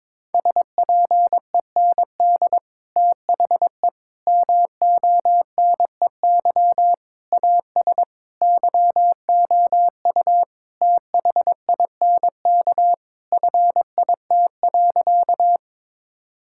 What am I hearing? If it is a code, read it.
Morse code, heard as a keyed tone, about 22 wpm, 700 Hz. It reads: SPEND THE MONEY AS YOU THINK FIT.
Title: Sentences from Top 1000 Words - Review Part 5 - 22wpm